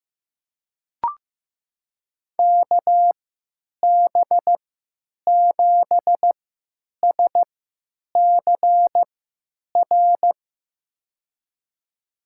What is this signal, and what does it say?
Morse code, heard as a keyed tone, about 15 wpm, spaced as for 9 wpm, 700 Hz. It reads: KB7SCR
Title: USA Call Signs - 3x Character Spacing - 15wpm